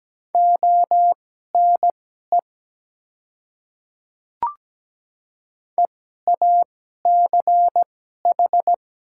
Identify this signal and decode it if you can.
Morse code, heard as a keyed tone, about 17 wpm, spaced as for 12 wpm, 700 Hz. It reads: ONE EACH
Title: 2x Character Spacing - Top 100 Words - 17wpm